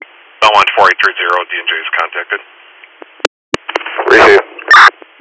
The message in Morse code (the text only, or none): none